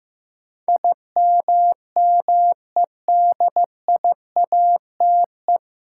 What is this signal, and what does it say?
Morse code, heard as a keyed tone, about 15 wpm, 700 Hz. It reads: IMMEDIATE
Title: Sentences from Top 1200 Words - New Word Drill Alt - 15wpm